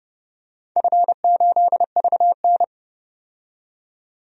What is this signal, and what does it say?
Morse code, heard as a keyed tone, about 31 wpm, 700 Hz. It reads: F8VN
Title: DXCC-100 Call Signs - 4 Characters - 30wpm